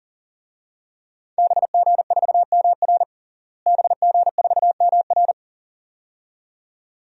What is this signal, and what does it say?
Morse code, heard as a keyed tone, about 37 wpm, 700 Hz. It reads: BG4MR BG4MR
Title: DXCC-100 Call Signs - 5 Characters - Repeated 2x - 40wpm